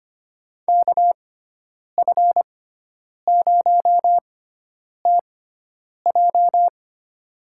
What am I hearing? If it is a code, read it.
Morse code, heard as a keyed tone, about 25 wpm, spaced as for 9 wpm, 700 Hz. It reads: KF0TJ
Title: USA Call Signs - 5 Characters - 6x Character Spacing - 25wpm